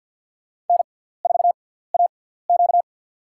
Morse code, heard as a keyed tone, about 52 wpm, spaced as for 17 wpm, 700 Hz. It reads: N4AQ
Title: USA Call Signs - 4 Characters - 6x Character Spacing - 50wpm